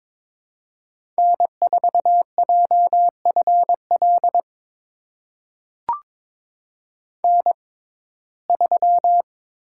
Morse code, heard as a keyed tone, about 22 wpm, spaced as for 16 wpm, 700 Hz. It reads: N4JFL N3
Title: USA Call Signs - 5 Characters - 6x Character Spacing - 22wpm